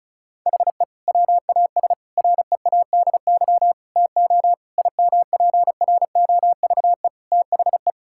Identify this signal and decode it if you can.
Morse code, heard as a keyed tone, about 36 wpm, 700 Hz. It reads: HE WAS READY TO IMPROVE THE
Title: Sentences from Top 1000 Words - Review Part 5 - 35wpm